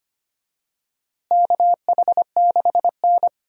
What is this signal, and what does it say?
Morse code, heard as a keyed tone, about 25 wpm, 700 Hz. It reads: KH6N
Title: DXCC-100 Call Signs - 4 Characters - Repeated 3x - 25wpm